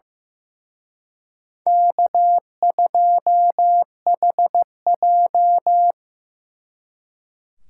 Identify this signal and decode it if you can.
Morse code, heard as a keyed tone, about 18 wpm, 700 Hz. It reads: K2HJ
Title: Speed Racing - USA Call Signs 15wpm